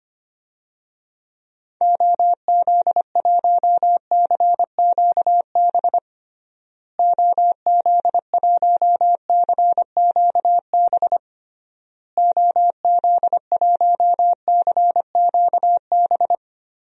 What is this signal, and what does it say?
Morse code, heard as a keyed tone, about 25 wpm, 700 Hz. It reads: OZ1CQB OZ1CQB OZ1CQB
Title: DXCC-100 Call Signs - 6 Characters - Repeated 3x - 25wpm